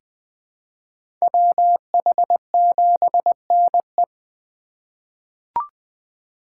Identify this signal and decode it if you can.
Morse code, heard as a keyed tone, about 20 wpm, 700 Hz. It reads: WH7NE
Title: USA Call Signs - 5 Characters - 8x Character Spacing - 20wpm